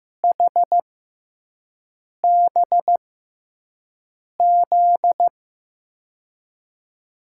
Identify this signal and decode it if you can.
Morse code, heard as a keyed tone, about 15 wpm, spaced as for 5 wpm, 700 Hz. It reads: HBZ
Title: USA Call Signs - 6 Characters - 6x Character Spacing - 15wpm